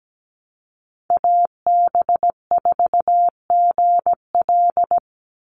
Morse code, heard as a keyed tone, about 17 wpm, 700 Hz. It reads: AB4GL
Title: USA Call Signs 17wpm